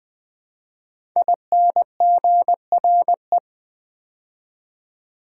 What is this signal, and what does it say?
Morse code, heard as a keyed tone, about 20 wpm, 700 Hz. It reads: INGRE
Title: Build A Word - 5 Letters At A Time - 10-Letter Words - 20wpm